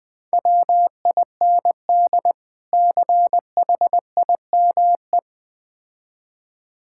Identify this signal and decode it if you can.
Morse code, heard as a keyed tone, about 20 wpm, 700 Hz. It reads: WIND CHIME
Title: Difficult Words with Hints - 20wpm